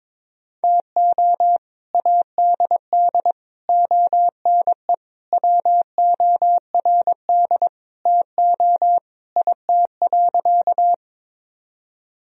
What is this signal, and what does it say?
Morse code, heard as a keyed tone, about 22 wpm, 700 Hz. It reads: TO ADD ONE WORD TO IT.
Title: Sentences from Top 1000 Words - Review Part 1 - 22wpm